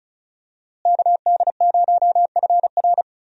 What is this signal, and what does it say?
Morse code, heard as a keyed tone, about 36 wpm, 700 Hz. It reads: KD0FR
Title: USA Call Signs - 5 Characters - Repeated 3x - 35wpm